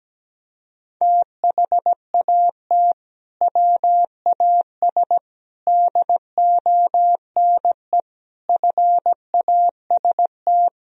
Morse code, heard as a keyed tone, about 17 wpm, 700 Hz. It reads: THAT WAS DONE FAST
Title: Sentences from Top 1000 Words - Review Part 2 - 17wpm